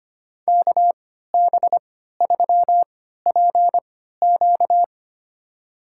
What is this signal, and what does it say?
Morse code, heard as a keyed tone, about 25 wpm, spaced as for 14 wpm, 700 Hz. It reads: KB3PQ
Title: USA Call Signs - 5 Characters - 3x Character Spacing - 25wpm